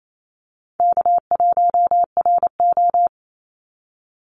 Morse code, heard as a keyed tone, about 28 wpm, 700 Hz. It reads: K1RO
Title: Easy Call Signs - 28wpm